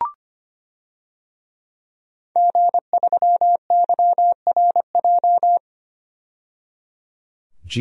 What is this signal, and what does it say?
Morse code, heard as a keyed tone, about 25 wpm, 700 Hz. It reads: G3YRJ
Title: DXCC-100 Call Signs - 5 Characters - 25wpm